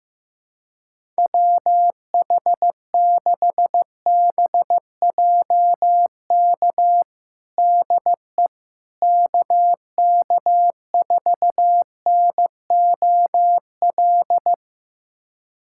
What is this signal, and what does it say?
Morse code, heard as a keyed tone, about 15 wpm, 700 Hz. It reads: WH6BJK DE KK4NOL
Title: USA Callsign DE Callsign 15wpm